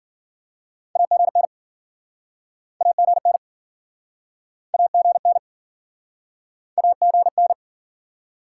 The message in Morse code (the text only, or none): AGN AGN AGN AGN